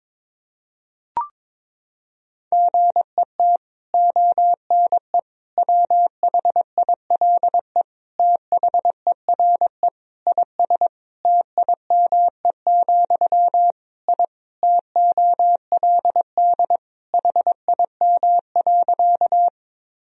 Morse code, heard as a keyed tone, about 22 wpm, 700 Hz. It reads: GET ONE WHILE THERE IS TIME, I TOLD HIM.
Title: Sentences from Top 1000 Words - Review Part 1 - 22wpm